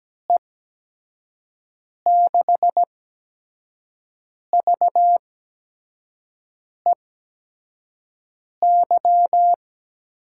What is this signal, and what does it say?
Morse code, heard as a keyed tone, about 17 wpm, spaced as for 5 wpm, 700 Hz. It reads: E6VEY
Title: USA Call Signs - 6 Characters - 8x Character Spacing - 17wpm